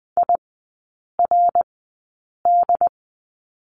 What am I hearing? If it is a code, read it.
Morse code, heard as a keyed tone, about 20 wpm, spaced as for 8 wpm, 700 Hz. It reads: IRD
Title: 138 Lesson 23e - Letter f - Three Characters 20wpm